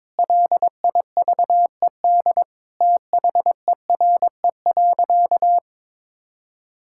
Morse code, heard as a keyed tone, about 22 wpm, 700 Hz. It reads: LIVED THERE.
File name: Sentences with 7-Word Phrases - 22wpm